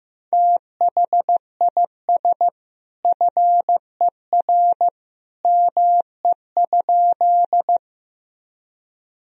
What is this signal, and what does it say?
Morse code, heard as a keyed tone, about 16 wpm, 700 Hz. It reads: THIS FER ME?
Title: Sentences from Top 200 Words and Abbreviations 15wpm